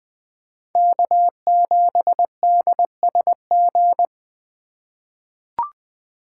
Morse code, heard as a keyed tone, about 20 wpm, 700 Hz. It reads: K7DSG
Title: USA Call Signs - 5 Characters - 4x Character Spacing - 20wpm